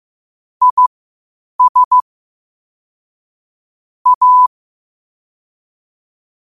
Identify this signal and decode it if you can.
Morse code, heard as a keyed tone, about 15 wpm, spaced as for 8 wpm, 1000 Hz. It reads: IS A
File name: qotdfiles / 8wpm